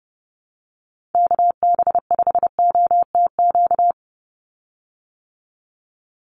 Morse code, heard as a keyed tone, about 30 wpm, 700 Hz. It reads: KB5OTQ
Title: USA Call Signs 30wpm